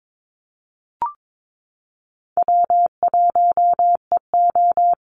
Morse code, heard as a keyed tone, about 22 wpm, 700 Hz. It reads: W1EO
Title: USA Call Signs Repeated 3x 22wpm